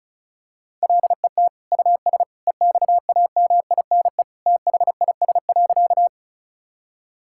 Morse code, heard as a keyed tone, about 34 wpm, 700 Hz. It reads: LET US EXAMINE THIS.
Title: Sentences from Top 1600 Words 35wpm